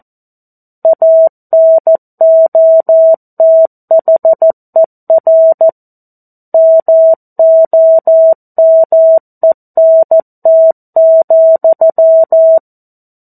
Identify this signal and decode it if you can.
Morse code, heard as a keyed tone, about 14 wpm, 650 Hz. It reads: ANOTHER MOMENT,
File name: 02-Chapter-02 14 WPM